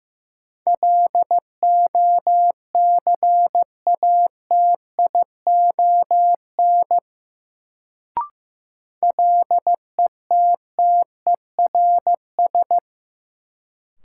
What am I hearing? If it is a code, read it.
Morse code, heard as a keyed tone, about 15 wpm, 700 Hz. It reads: LOCATION LETTERS
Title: Sentences from Top 1000 Words - Review Part 6 - New Word Drill - 15wpm